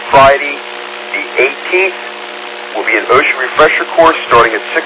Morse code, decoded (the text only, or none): none